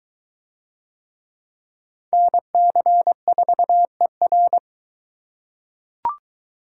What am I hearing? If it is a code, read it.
Morse code, heard as a keyed tone, about 22 wpm, 700 Hz. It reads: NC4ER